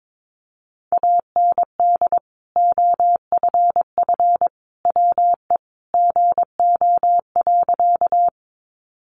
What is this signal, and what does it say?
Morse code, heard as a keyed tone, about 22 wpm, 700 Hz. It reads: AND OFF WE GO.
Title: Sentences from Top 200 Words 22wpm